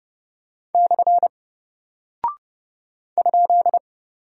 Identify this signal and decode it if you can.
Morse code, heard as a keyed tone, about 30 wpm, 700 Hz. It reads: / ?
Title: Single Character (Letter:Number:Punctuation:Prosign) Alt 30wpm